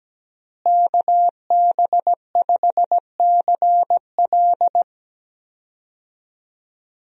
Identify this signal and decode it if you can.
Morse code, heard as a keyed tone, about 17 wpm, 700 Hz. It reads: KB5CL
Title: USA Call Signs - 5 Characters - Repeated 3x - 17wpm